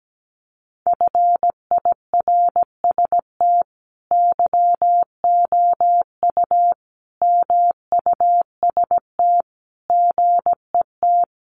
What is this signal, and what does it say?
Morse code, heard as a keyed tone, about 17 wpm, 700 Hz. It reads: FIRST YOU MUST GET
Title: Sentences from Top 1500 Words 17wpm